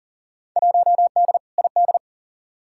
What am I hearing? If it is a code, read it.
Morse code, heard as a keyed tone, about 40 wpm, 700 Hz. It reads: 1D ID
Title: Field Day Exchanges - 40wpm